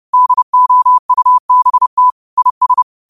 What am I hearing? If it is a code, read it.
Morse code, heard as a keyed tone, about 30 wpm, 1000 Hz. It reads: DOUBT IS
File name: qotdfiles / 30wpm